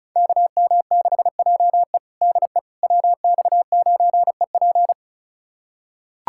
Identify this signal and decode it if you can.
Morse code, heard as a keyed tone, about 36 wpm, 700 Hz. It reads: KM6JE DE WX9EP
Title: USA Callsign DE Callsign 35wpm